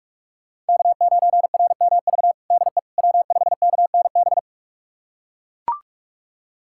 Morse code, heard as a keyed tone, about 43 wpm, 700 Hz. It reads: K9RMV DE W5KNB
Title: USA Callsign DE Callsign - 5 Characters - 45wpm